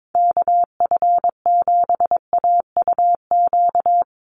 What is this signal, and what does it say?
Morse code, heard as a keyed tone, about 22 wpm, 700 Hz. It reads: KF7AUQ
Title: USA Call Signs 22wpm